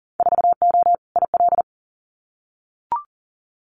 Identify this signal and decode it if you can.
Morse code, heard as a keyed tone, about 40 wpm, 700 Hz. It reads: 4O IL
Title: Winter Field Day Exchanges Encore 40wpm